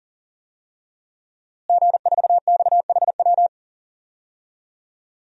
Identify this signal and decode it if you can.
Morse code, heard as a keyed tone, about 40 wpm, 700 Hz. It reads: G4XHW
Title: DXCC-100 Call Signs - 5 Characters - Repeated 2x - 40wpm